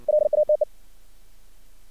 Morse code, heard as a keyed tone, about 59 wpm, 610 Hz. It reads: LATE
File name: archived / CRSnet